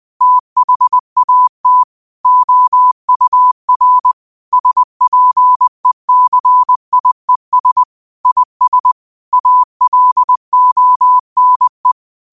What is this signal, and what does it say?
Morse code, heard as a keyed tone, about 20 wpm, 1000 Hz. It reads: THAT OUR SPECIES IS ALONE